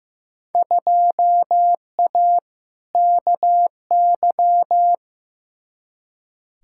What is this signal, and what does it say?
Morse code, heard as a keyed tone, about 15 wpm, 700 Hz. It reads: 2A KY
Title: Field Day Exchanges - 15wpm